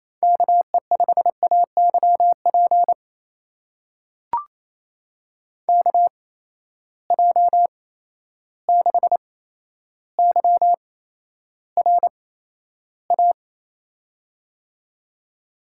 Morse code, heard as a keyed tone, about 28 wpm, spaced as for 12 wpm, 700 Hz. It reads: KE5AYP KJ6YRA
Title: USA Call Signs - 6 Characters - 8x Character Spacing - 28wpm